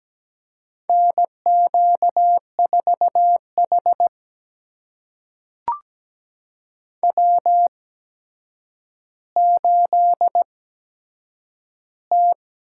Morse code, heard as a keyed tone, about 17 wpm, spaced as for 8 wpm, 700 Hz. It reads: NQ4H W8T
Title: USA Call Signs - 8x Character Spacing - 17wpm